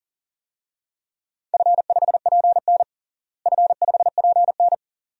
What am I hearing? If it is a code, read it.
Morse code, heard as a keyed tone, about 40 wpm, 700 Hz. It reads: F5PN F5PN
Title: DXCC-100 Call Signs - 4 Characters - Repeated 2x - 40wpm